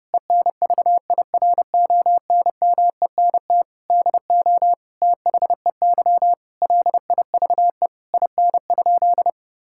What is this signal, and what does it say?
Morse code, heard as a keyed tone, about 32 wpm, 700 Hz. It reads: ENVIRONMENT DO THEY LIVE IN?
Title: Sentences from Top 1000 Words - Review Part 5 - 30wpm